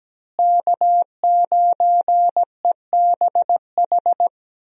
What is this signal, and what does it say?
Morse code, heard as a keyed tone, about 17 wpm, 700 Hz. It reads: K9EBH
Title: USA Call Signs - 5 Characters - Repeated 2x - 17wpm